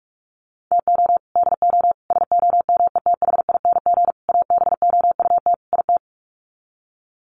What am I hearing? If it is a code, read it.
Morse code, heard as a keyed tone, about 44 wpm, 700 Hz. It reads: TO DO SOMETHING ABOUT IT
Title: Sets of 5 Words 45wpm